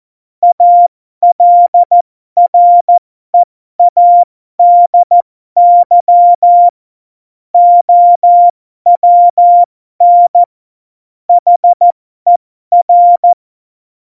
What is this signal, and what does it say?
Morse code, heard as a keyed tone, about 14 wpm, spaced as for 12 wpm, 700 Hz. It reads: ALREADY OWN HER
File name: Morsekurs / Witze